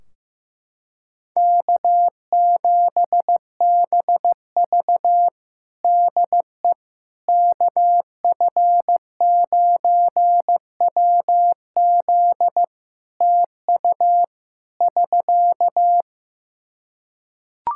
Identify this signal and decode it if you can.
Morse code, heard as a keyed tone, about 16 wpm, 700 Hz. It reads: K7BV DE KF9WZ TU <SK>
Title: K1USN SST QSO Practice 15wpm